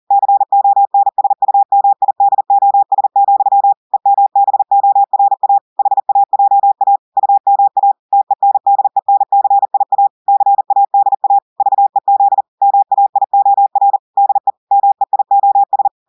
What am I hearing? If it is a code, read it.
Morse code, heard as a keyed tone, about 40 wpm, 800 Hz. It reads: CONSUMIDOS, EMBORA HAJA UMA TENDENCIA CADA VEZ MAIOR DE MEIOS
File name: audios / portugais / portugais texte en clair 2 vitesse 40 mots minutes